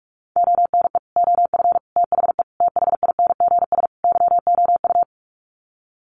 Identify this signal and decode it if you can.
Morse code, heard as a keyed tone, about 44 wpm, 700 Hz. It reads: ONE OF THE THINGS YOU
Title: Sets of 5 Words Encore2 45wpm